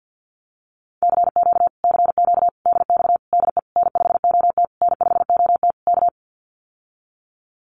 Morse code, heard as a keyed tone, about 50 wpm, 700 Hz. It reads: CQ CQ DX DE N5OT N5OT K
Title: Common QSO Phrases Encore 50wpm